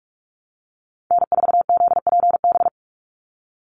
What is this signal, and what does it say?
Morse code, heard as a keyed tone, about 44 wpm, 700 Hz. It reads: N4ZPB
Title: USA Call Signs 45wpm